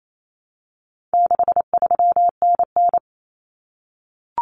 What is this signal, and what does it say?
Morse code, heard as a keyed tone, about 28 wpm, 700 Hz. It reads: 63NN
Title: Common Cut Numbers - 4-Digit 28wpm